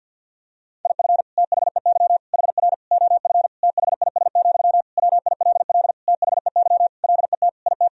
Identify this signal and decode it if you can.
Morse code, heard as a keyed tone, about 52 wpm, 700 Hz. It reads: IF THEY HR OF THIS, WILL THEY LET IT
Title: Sentences from Top 400 Words and Abbreviations 50wpm